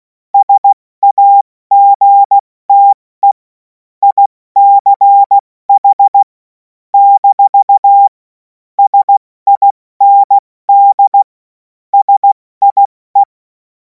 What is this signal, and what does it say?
Morse code, heard as a keyed tone, about 16 wpm, 800 Hz. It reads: SAGTE ICH - SIND SIE
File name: Morsekurs / Witze / Witz 07. April 2026 70 F80